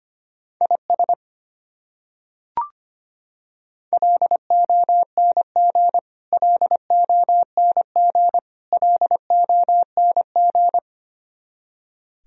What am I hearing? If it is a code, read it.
Morse code, heard as a keyed tone, about 25 wpm, 700 Hz. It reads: IS LONG LONG LONG
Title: Top 100 Words Repeated 25wpm